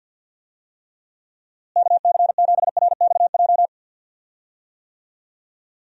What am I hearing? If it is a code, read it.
Morse code, heard as a keyed tone, about 52 wpm, 700 Hz. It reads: KC7RXJ
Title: USA Call Signs - 6 Characters - Repeated 3x - 50wpm